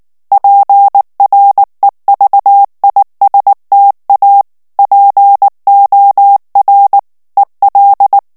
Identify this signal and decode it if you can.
Morse code, heard as a keyed tone, about 19 wpm, 800 Hz. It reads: PREVISTA POR EL
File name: morse mp3 / ESPAGNOL TEXTE EN CLAIR 2 VITESSE 18 MOTS MINUTES